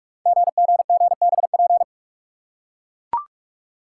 Morse code, heard as a keyed tone, about 44 wpm, 690 Hz. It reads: GGGBP
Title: Code Groups - 5 Letters - 45wpm